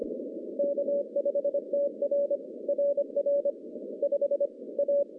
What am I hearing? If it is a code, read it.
Morse code, heard as a keyed tone, about 25 wpm, 570 Hz. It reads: K5TR RR 5 A
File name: contests / CW-SS-k5tr- -0100